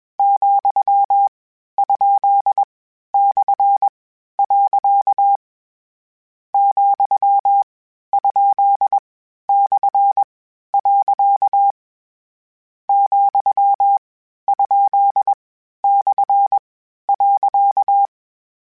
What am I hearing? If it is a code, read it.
Morse code, heard as a keyed tone, about 21 wpm, spaced as for 12 wpm, 800 Hz. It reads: ,?/. ,?/. ,?/.
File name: Misc / Misc-mp3s / Punct @20